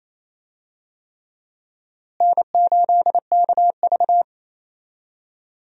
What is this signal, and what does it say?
Morse code, heard as a keyed tone, about 28 wpm, 700 Hz. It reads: N8KV
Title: USA Call Signs - 4 Characters - Repeated 3x - 28wpm